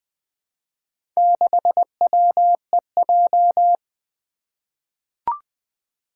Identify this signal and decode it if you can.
Morse code, heard as a keyed tone, about 20 wpm, 700 Hz. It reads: 6WEJ
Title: Four Challenging Letter-Number Combinations 20wpm